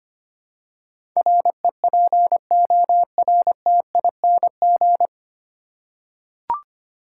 Morse code, heard as a keyed tone, about 25 wpm, 700 Hz. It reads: REPORTING
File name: Nine Letter Words - 25wpm